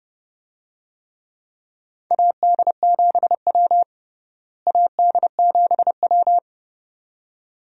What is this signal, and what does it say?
Morse code, heard as a keyed tone, about 30 wpm, 700 Hz. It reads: AD7W AD7W
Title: USA Call Signs - 4 Characters - Repeated 2x - 30wpm